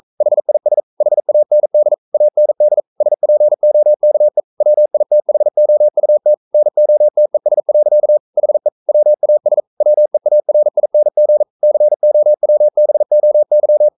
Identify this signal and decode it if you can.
Morse code, heard as a keyed tone, about 42 wpm, 600 Hz. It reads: HIS HAND AND SPOKE WITHOUT NOTES. HE WAS WEARING COWBOY
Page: Romanen Never af Ken Follet på engelsk.